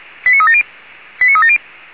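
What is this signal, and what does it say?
Five-tone selective call (ZVEI), heard as quick runs of five tones, tones of about 70 ms. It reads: 89370 89370
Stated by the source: ZVEI five-tone call 89360 received on an SDR, channel 464.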